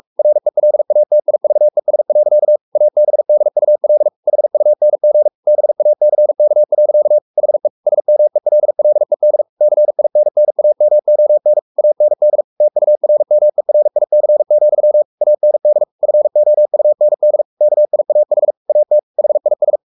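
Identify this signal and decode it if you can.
Morse code, heard as a keyed tone, about 44 wpm, 600 Hz. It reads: RELATIVES. ABDUL HUNG BACK. HE SMELLED CINNAMON AND TURMERIC, AND FOUND KIAH AT HIS
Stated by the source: Morse code at 43 wpm sending Ken Follett's novel Never in English, from CW med Gnister.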